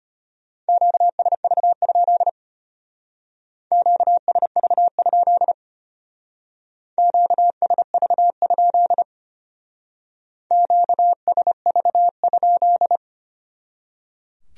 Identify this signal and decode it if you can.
Morse code, heard as a keyed tone, about 31 wpm, 700 Hz. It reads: QSV? QSV? QSV? QSV?
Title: Speed-Racing Q-Codes - 25wpm